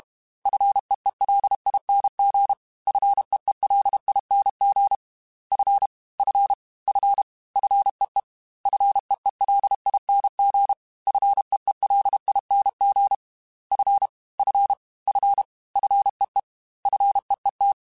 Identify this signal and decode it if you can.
Morse code, heard as a keyed tone, about 32 wpm, 800 Hz. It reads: FEELING FEELING F F F FEE FEELING FEELING F F F FEE FEET
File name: F and L / F and L mp3 / F and L Prefixes 3x2 Twice @30